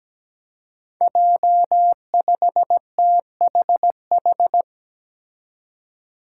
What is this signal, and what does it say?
Morse code, heard as a keyed tone, about 17 wpm, 700 Hz. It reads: J5THH
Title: Five Challenging Letter-Number Combinations 17wpm